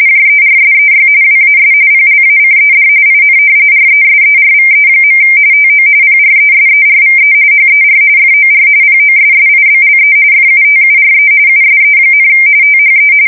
RTTY, about 45.45 baud. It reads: JERSEY WAS PLACED IN RESERVE. TO PRESERVE HER FOR POSSIBLE FUTURE USE, THE
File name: fsk170 / battleshipnj